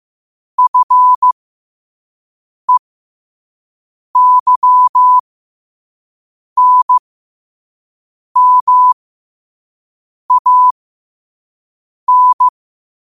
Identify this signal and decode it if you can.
Morse code, heard as a keyed tone, about 15 wpm, spaced as for 5 wpm, 1000 Hz. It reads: FEYNMAN
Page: qotd-at-5-wpm